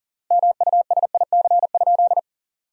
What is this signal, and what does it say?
Morse code, heard as a keyed tone, about 40 wpm, 700 Hz. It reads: MUSIC?
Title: 4x Word-Spacing - Sentences from Top 300 Words - 40wpm